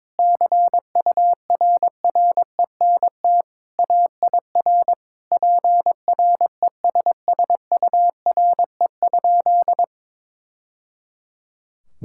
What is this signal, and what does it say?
Morse code, heard as a keyed tone, about 22 wpm, 700 Hz. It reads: CURRENT AIR PRESSURE?
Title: Sentences from Top 1000 Words - Review Part 6 - 22wpm